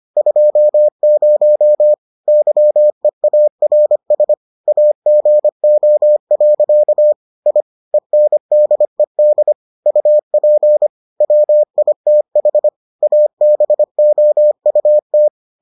Morse code, heard as a keyed tone, about 25 wpm, 600 Hz. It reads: 20 YEARS AGO. I ENDED UP WITH ABOUT